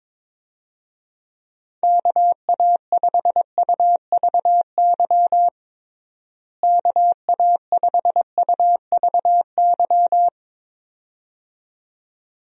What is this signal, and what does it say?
Morse code, heard as a keyed tone, about 22 wpm, 700 Hz. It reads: KA5UVY KA5UVY
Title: USA Call Signs - 6 Characters - Repeated 2x - 22wpm